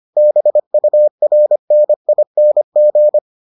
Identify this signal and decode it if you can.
Morse code, heard as a keyed tone, about 25 wpm, 600 Hz. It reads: BURNING